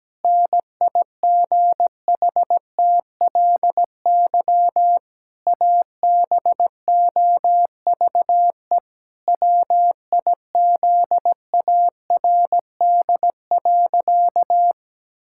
Morse code, heard as a keyed tone, about 17 wpm, 700 Hz. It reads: NIGHTLY ABOVE WIZARD.
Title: Pangram Challenge v2 - 17wpm